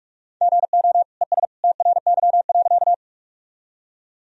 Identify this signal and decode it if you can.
Morse code, heard as a keyed tone, about 45 wpm, 700 Hz. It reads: GO ES TRY.
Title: Sentences from Top 200 Words and Abbreviations 45wpm